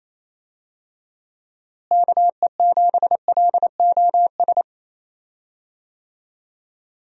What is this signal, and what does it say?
Morse code, heard as a keyed tone, about 28 wpm, 700 Hz. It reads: KE7LOS
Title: USA Call Signs - 6 Characters - Repeated 3x - 28wpm